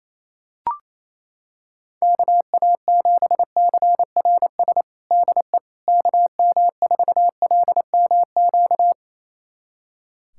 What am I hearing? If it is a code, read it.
Morse code, heard as a keyed tone, about 28 wpm, 700 Hz. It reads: KA7CRS DE KM4LMQ
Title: USA Callsign DE Callsign 28wpm